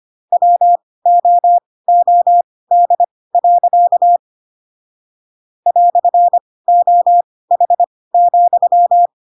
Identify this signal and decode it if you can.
Morse code, heard as a keyed tone, about 25 wpm, spaced as for 18 wpm, 700 Hz. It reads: WOOD. "OH,
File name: Pinocchio / Pinocchio ch1 12 25 ik1hge